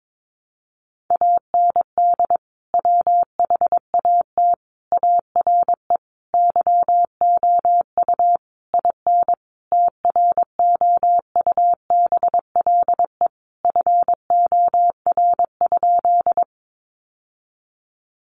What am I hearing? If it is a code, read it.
Morse code, heard as a keyed tone, about 22 wpm, 700 Hz. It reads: AND WHAT ARE YOU IN TROUBLE FOR?
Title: Sentences from Top 1200 Words 22wpm